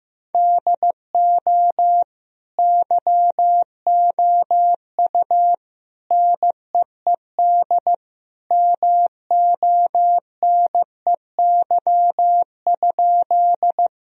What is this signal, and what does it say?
Morse code, heard as a keyed tone, about 16 wpm, 700 Hz. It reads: DO YOU NEED MONEY?
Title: Sentences from Top 1000 Words - Review Part 1 - 15wpm